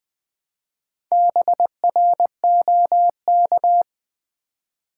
Morse code, heard as a keyed tone, about 20 wpm, 700 Hz. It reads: BROK
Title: Build A Word - 2 Letters At A Time - 6-Letter Words - 20wpm